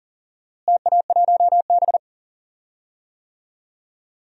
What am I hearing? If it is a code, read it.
Morse code, heard as a keyed tone, about 40 wpm, 700 Hz. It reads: TA1B
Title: DXCC-100 Call Signs - 4 Characters - Repeated 3x - 40wpm